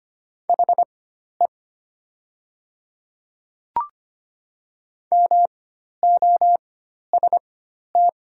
Morse code, heard as a keyed tone, about 25 wpm, spaced as for 12 wpm, 700 Hz. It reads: HE MOST
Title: 4x Character Spacing - Top 100 Words - 25wpm